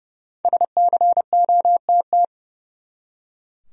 Morse code, heard as a keyed tone, about 30 wpm, 700 Hz. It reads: SCOTT
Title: USA Common Names - 30wpm